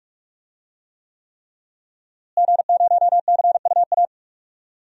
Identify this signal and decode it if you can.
Morse code, heard as a keyed tone, about 46 wpm, 700 Hz. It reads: G0CUA
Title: DXCC-100 Call Signs - 5 Characters - Repeated 3x - 45wpm